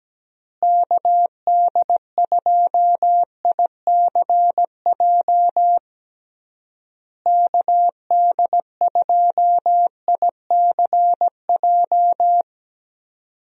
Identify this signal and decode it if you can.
Morse code, heard as a keyed tone, about 17 wpm, 700 Hz. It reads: KD2ICJ KD2ICJ
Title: USA Call Signs - 6 Characters - Repeated 3x - 17wpm